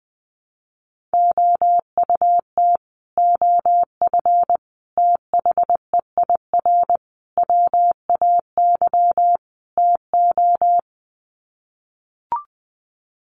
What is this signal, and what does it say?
Morse code, heard as a keyed tone, about 20 wpm, 700 Hz. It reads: OUT OF THEIR WAY TO
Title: Sets of 5 Words Encore2 20wpm